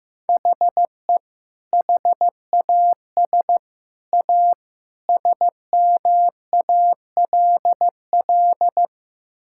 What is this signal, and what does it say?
Morse code, heard as a keyed tone, about 15 wpm, 700 Hz. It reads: HE HAS A SMALL
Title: Sentences from Top 1000 Words - Review Part 6 - 15wpm